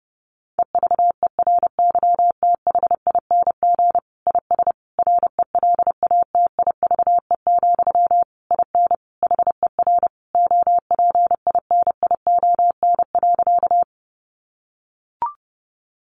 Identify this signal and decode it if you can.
Morse code, heard as a keyed tone, about 30 wpm, 700 Hz. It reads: EVERYTHING IS RELATIVE, IN HER OPINION.
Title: Sentences from Top 1400 Words 30wpm